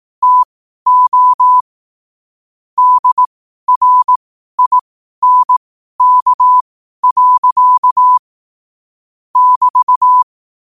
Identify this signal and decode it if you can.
Morse code, heard as a keyed tone, about 18 wpm, spaced as for 12 wpm, 1000 Hz. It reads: TO DRINK. <BT>
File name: qotdfiles / 12wpm